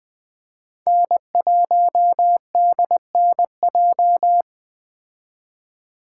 Morse code, heard as a keyed tone, about 20 wpm, 700 Hz. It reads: N1DNJ
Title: Speed Racing - USA Call Signs 17wpm